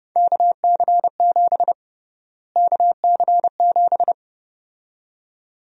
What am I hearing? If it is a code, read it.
Morse code, heard as a keyed tone, about 31 wpm, 700 Hz. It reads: KC7 KC7
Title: USA Call Sign Prefixes - Repeated 2x - 30wpm